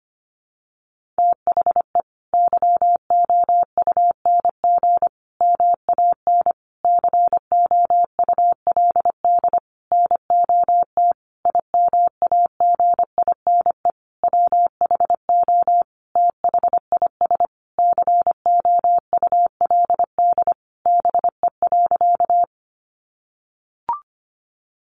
Morse code, heard as a keyed tone, about 25 wpm, 700 Hz. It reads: THE YOUNG MAN COULD NOT IMAGINE WHO THIS COULD BE.
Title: Sentences from Top 1400 Words 25wpm